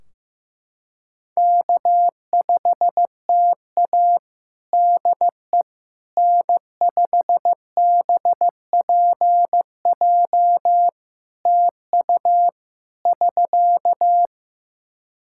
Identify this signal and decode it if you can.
Morse code, heard as a keyed tone, about 15 wpm, 700 Hz. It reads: K5TA DE N5BPJ TU <SK>
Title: K1USN SST QSO Practice 15wpm